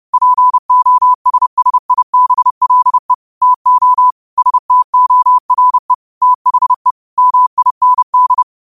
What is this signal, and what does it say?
Morse code, heard as a keyed tone, about 30 wpm, 1000 Hz. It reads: POSSIBLE TO STORE THE MIND
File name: qotdfiles / 30wpm